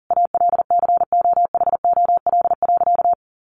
Morse code, heard as a keyed tone, about 40 wpm, 700 Hz. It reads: ALCOHOL.
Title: Sentences from Top 1400 Words 40wpm